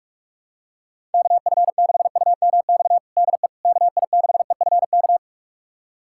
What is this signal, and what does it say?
Morse code, heard as a keyed tone, about 46 wpm, 700 Hz. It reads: KF6UMX DE KI6EFK
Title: USA Callsign DE Callsign - 6 Characters - 45wpm